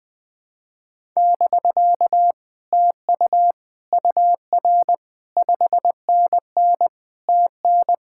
Morse code, heard as a keyed tone, about 20 wpm, 700 Hz. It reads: <BK> TU UR 5NN TN
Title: Speed-Racing Common QSO Phrases - 20wpm